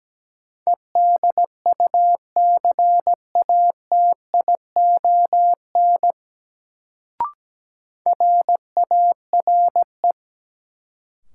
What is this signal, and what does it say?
Morse code, heard as a keyed tone, about 17 wpm, 700 Hz. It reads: EDUCATION RARE
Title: Sentences from Top 1100 Words - New Word Drill Alt - 17wpm